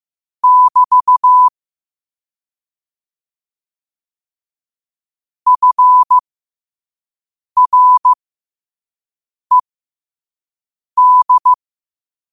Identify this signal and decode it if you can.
Morse code, heard as a keyed tone, about 15 wpm, spaced as for 5 wpm, 1000 Hz. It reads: <BT> FRED
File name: qotdfiles / 5wpm